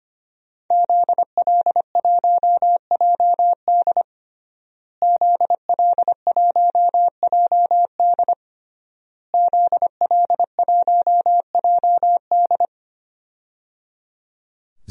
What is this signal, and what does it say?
Morse code, heard as a keyed tone, about 25 wpm, 700 Hz. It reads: ZL1JD ZL1JD ZL1JD
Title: DXCC-100 Call Signs - 5 Characters - Repeated 3x - 25wpm